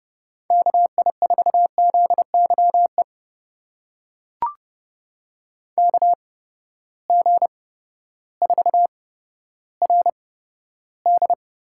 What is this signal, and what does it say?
Morse code, heard as a keyed tone, about 30 wpm, spaced as for 14 wpm, 700 Hz. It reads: KI4ZYE KG4RD
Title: USA Call Signs - 6 Characters - 8x Character Spacing - 30wpm